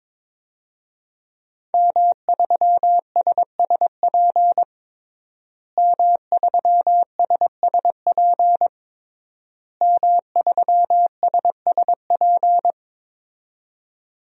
Morse code, heard as a keyed tone, about 22 wpm, 700 Hz. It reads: M3SSP M3SSP M3SSP
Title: DXCC-100 Call Signs - 5 Characters - Repeated 3x - 22wpm